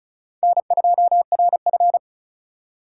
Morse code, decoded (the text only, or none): N2RF